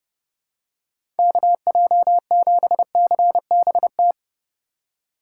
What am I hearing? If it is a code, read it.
Morse code, heard as a keyed tone, about 30 wpm, 700 Hz. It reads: KJ7CBT
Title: USA Call Signs - 6 Characters - 2x Character Spacing - 30wpm